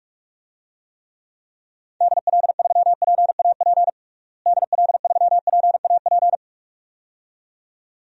Morse code, heard as a keyed tone, about 46 wpm, 700 Hz. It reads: DL3PAP DL3PAP
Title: DXCC-100 Call Signs - 6 Characters - Repeated 2x - 45wpm